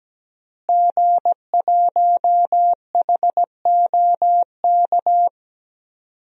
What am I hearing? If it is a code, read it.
Morse code, heard as a keyed tone, about 17 wpm, 700 Hz. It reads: G1HOK
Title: DXCC-100 Call Signs - 5 Characters - Repeated 3x - 17wpm